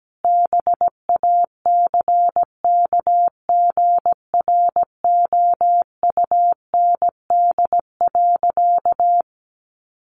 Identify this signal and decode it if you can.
Morse code, heard as a keyed tone, about 17 wpm, 700 Hz. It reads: BACKGROUND.
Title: Sentences from Top 1500 Words 17wpm